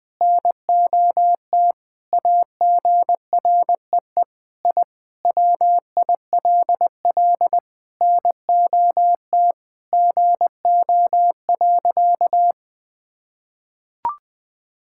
Morse code, heard as a keyed tone, about 20 wpm, 700 Hz. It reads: NOT AGREE I WILL NOT GO.
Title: Sentences with 4-Word Phrases - 20wpm